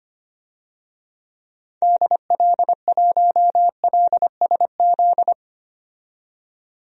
Morse code, heard as a keyed tone, about 25 wpm, 700 Hz. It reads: DL1LSZ
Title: DXCC-100 Call Signs - 6 Characters - 25wpm